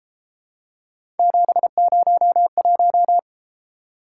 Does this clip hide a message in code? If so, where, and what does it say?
Yes: Morse code from 1.2 to 3.2 s, heard as a keyed tone, about 30 wpm, 700 Hz. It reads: 701